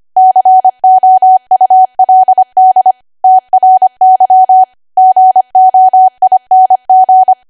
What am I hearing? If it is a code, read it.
Morse code, heard as a keyed tone, about 25 wpm, 750 Hz. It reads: COULD TRY GOING
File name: Archive / 25 WPM / 25WPM